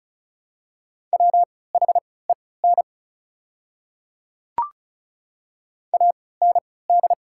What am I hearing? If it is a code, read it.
Morse code, heard as a keyed tone, about 36 wpm, spaced as for 20 wpm, 700 Hz. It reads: WHEN AND
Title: 3x Character Spacing - Top 200 Words - 35wpm